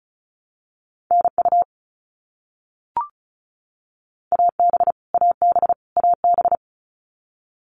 Morse code, heard as a keyed tone, about 34 wpm, 700 Hz. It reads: NU AB AB AB
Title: Canadian Provinces and Territories - Repeated 3x - 35wpm